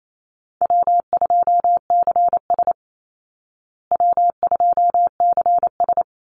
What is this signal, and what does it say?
Morse code, heard as a keyed tone, about 28 wpm, 700 Hz. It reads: W2CS W2CS
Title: Call Signs Repeated 3x - 28wpm